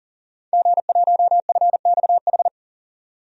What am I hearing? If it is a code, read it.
Morse code, heard as a keyed tone, about 40 wpm, 700 Hz. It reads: G1FXH
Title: DXCC-100 Call Signs - 5 Characters - 40wpm